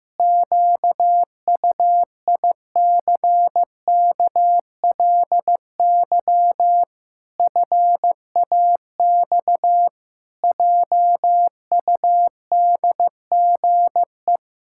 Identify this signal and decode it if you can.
Morse code, heard as a keyed tone, about 15 wpm, 700 Hz. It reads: QUICKLY FAX JUDGE
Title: Pangram Challenge v2 - 15wpm